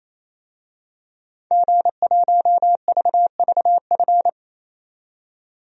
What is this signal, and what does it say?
Morse code, heard as a keyed tone, about 28 wpm, 700 Hz. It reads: G1VVF
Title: DXCC-100 Call Signs - 5 Characters - 28wpm